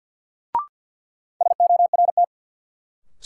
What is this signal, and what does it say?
Morse code, heard as a keyed tone, about 52 wpm, 700 Hz. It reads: SORT
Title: Sets of 2 Words - New Word Drill - 50wpm